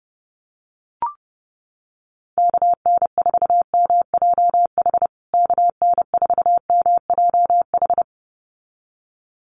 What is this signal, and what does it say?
Morse code, heard as a keyed tone, about 30 wpm, 700 Hz. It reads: KN4MJH KN4MJH
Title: USA Call Signs Repeated 2x 30wpm